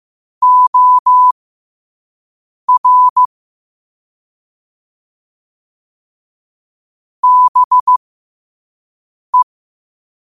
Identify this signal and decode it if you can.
Morse code, heard as a keyed tone, about 15 wpm, spaced as for 5 wpm, 1000 Hz. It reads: OR BE
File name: 5wpm